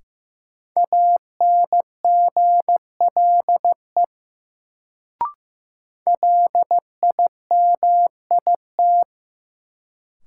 Morse code, heard as a keyed tone, about 15 wpm, 700 Hz. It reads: ANGLE LIMIT
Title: Sentences from Top 1500 Words - New Word Drill Alt - 15wpm